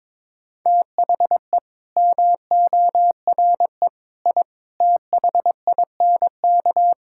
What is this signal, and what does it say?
Morse code, heard as a keyed tone, about 22 wpm, 700 Hz. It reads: THE MORE I THINK
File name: practice / 22wpm / Sets of 8 Words - 22wpm